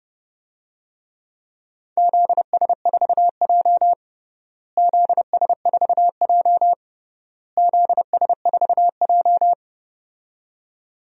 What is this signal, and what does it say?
Morse code, heard as a keyed tone, about 30 wpm, 700 Hz. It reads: ZS4J ZS4J ZS4J
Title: DXCC-100 Call Signs - 4 Characters - Repeated 3x - 30wpm